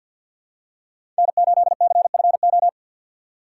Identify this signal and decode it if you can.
Morse code, heard as a keyed tone, about 52 wpm, 700 Hz. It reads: N8CFO
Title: USA Call Signs - 5 Characters - 3x Character Spacing - 50wpm